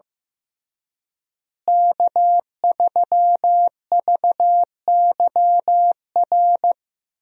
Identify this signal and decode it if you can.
Morse code, heard as a keyed tone, about 15 wpm, 700 Hz. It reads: K3VYR
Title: USA Call Signs - 5 Characters - Repeated 3x - 15wpm